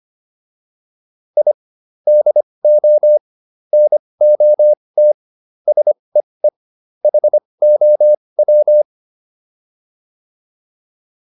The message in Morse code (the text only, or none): I DO NOT SEE HOW